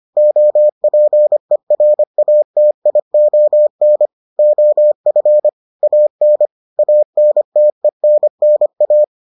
Morse code, heard as a keyed tone, about 25 wpm, 600 Hz. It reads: OPERATION OF AN ANTENNA